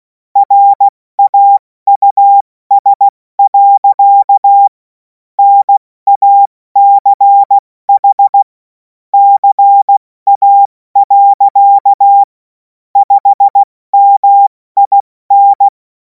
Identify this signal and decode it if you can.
Morse code, heard as a keyed tone, about 16 wpm, 800 Hz. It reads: RAUS. NACH CA. 5MIN
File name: Morsekurs / Witze / Witz 17. März 2026 70 F80